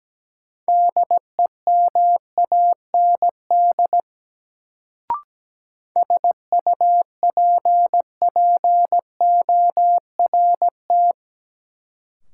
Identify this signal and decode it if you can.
Morse code, heard as a keyed tone, about 17 wpm, 700 Hz. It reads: DEMAND SUPPORT
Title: Sentences from Top 800 Words - New Word Drill - 17wpm